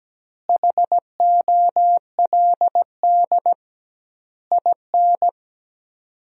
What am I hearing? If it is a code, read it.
Morse code, heard as a keyed tone, about 17 wpm, 700 Hz. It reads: HOLD IN
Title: 2x Word-Spacing - Sentences from Top 400 Words - 17wpm